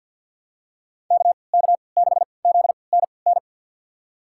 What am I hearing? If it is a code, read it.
Morse code, heard as a keyed tone, about 52 wpm, spaced as for 29 wpm, 700 Hz. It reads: KK6ZNN
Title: USA Call Signs - 3x Character Spacing - 50wpm